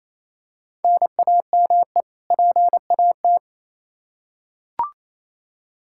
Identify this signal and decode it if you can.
Morse code, heard as a keyed tone, about 28 wpm, 700 Hz. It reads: NAME PAT
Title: Practice QSO - Exchange 1 - Beginning - 28wpm